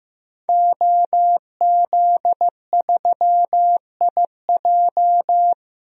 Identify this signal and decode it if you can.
Morse code, heard as a keyed tone, about 16 wpm, 700 Hz. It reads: OZ3IJ
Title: DXCC-100 Call Signs - 5 Characters - Repeated 3x - 15wpm